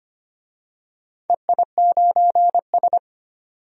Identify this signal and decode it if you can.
Morse code, heard as a keyed tone, about 25 wpm, 700 Hz. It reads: EI9S
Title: DXCC-100 Call Signs - 4 Characters - 25wpm